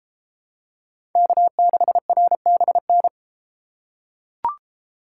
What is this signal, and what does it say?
Morse code, heard as a keyed tone, about 31 wpm, 700 Hz. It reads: K6RBN